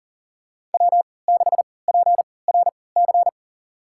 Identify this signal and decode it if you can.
Morse code, heard as a keyed tone, about 38 wpm, spaced as for 22 wpm, 700 Hz. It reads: W6PRC
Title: USA Call Signs - 5 Characters - 3x Character Spacing - 40wpm